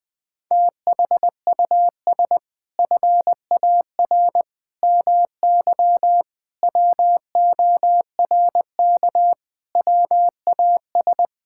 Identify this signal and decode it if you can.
Morse code, heard as a keyed tone, about 20 wpm, 700 Hz. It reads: THUS FAR MY WORK WAS
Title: Sentences from Top 1000 Words - Review Part 5 - 20wpm